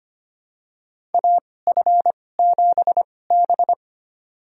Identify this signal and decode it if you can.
Morse code, heard as a keyed tone, about 25 wpm, spaced as for 18 wpm, 700 Hz. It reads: AF7B
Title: USA Call Signs - 4 Characters - 2x Character Spacing - 25wpm